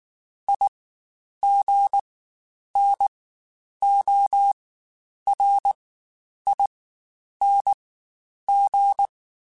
Morse code, heard as a keyed tone, about 19 wpm, spaced as for 9 wpm, 800 Hz. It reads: IGNORING
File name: Quotes Spaced / Quotes Spaced-mp3s / IgnoringFacts sp @18